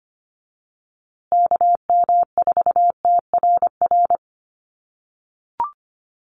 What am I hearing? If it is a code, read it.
Morse code, heard as a keyed tone, about 25 wpm, 700 Hz. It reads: KM4TRR
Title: USA Call Signs Repeated 2x 25wpm